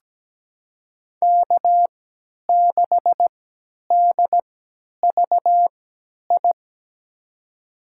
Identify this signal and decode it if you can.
Morse code, heard as a keyed tone, about 17 wpm, spaced as for 10 wpm, 700 Hz. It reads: K6DVI
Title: USA Call Signs - 3x Character Spacing - 17wpm